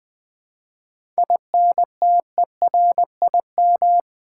Morse code, heard as a keyed tone, about 20 wpm, 700 Hz. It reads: INTERIM
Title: 7 Letter Words - First 3 Letters Given - 20wpm